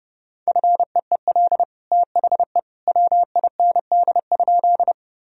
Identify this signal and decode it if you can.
Morse code, heard as a keyed tone, about 30 wpm, 700 Hz. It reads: FEEL THE WIND?
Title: Sentences from Top 1000 Words - Review Part 2 - 30wpm